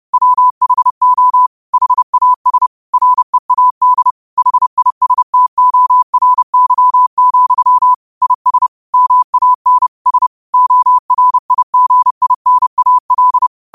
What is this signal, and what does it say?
Morse code, heard as a keyed tone, about 30 wpm, 1000 Hz. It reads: WHO HAS READ HISTORY, IS MAN S ORIGINAL
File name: qotdfiles / 30wpm